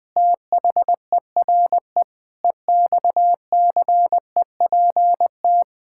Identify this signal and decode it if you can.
Morse code, heard as a keyed tone, about 21 wpm, 700 Hz. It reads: THERE EXCEPT
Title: Sentences from Top 1600 Words 20wpm